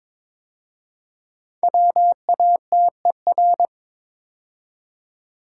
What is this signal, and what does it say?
Morse code, heard as a keyed tone, about 23 wpm, 700 Hz. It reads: WATER
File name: 5 Letter Words - First 3 Letters Given - 22wpm